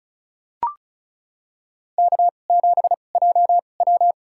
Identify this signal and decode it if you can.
Morse code, heard as a keyed tone, about 34 wpm, spaced as for 25 wpm, 700 Hz. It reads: K7JW
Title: USA Call Signs - 2x Character Spacing - 35wpm